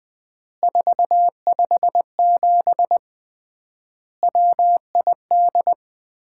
Morse code, heard as a keyed tone, about 20 wpm, 700 Hz. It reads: 457 WID
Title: Common QSO Phrases - 3x Word-Spacing - 20wpm